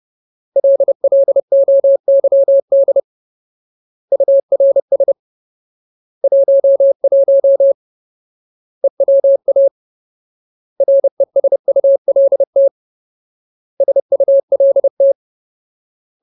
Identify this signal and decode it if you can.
Morse code, heard as a keyed tone, about 30 wpm, 550 Hz. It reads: LLOYD URS 11 EWA RESULT SULT
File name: Ultimate Word List / 30WPM Ultimate